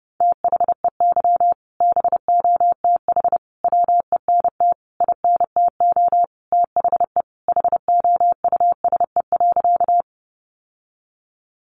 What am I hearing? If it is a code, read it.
Morse code, heard as a keyed tone, about 30 wpm, 700 Hz. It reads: THEY BOTH WENT INTO THE HOUSE.
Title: Sentences from Top 600 Words 30wpm